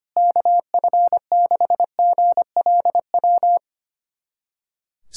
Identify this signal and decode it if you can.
Morse code, heard as a keyed tone, about 25 wpm, 700 Hz. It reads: KF6GLW
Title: SOTA QSO Practice 25wpm